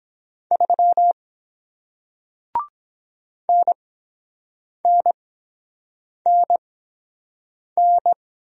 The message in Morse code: 3 NNNN

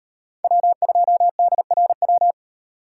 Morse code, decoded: W2DRW

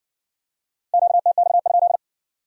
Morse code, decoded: QTX?

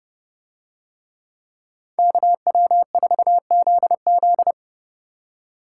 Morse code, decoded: KW4ZZ

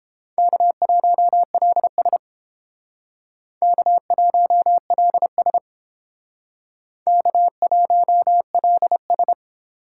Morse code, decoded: K1LS K1LS K1LS

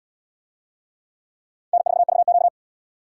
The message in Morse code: N4VQ

